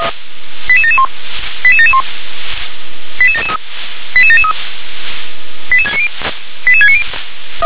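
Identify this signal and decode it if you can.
Five-tone selective call (ZVEI), heard as quick runs of five tones, tones of about 70 ms. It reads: 88771 88771 88773 88773 88690 88690
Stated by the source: ZVEI five-tone selective call, address 88773, received by SDR.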